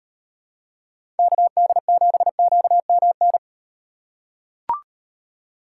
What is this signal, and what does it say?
Morse code, heard as a keyed tone, about 38 wpm, 700 Hz. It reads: KD7QMN